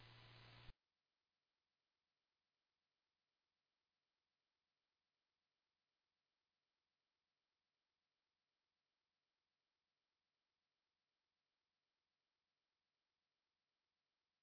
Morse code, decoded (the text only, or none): none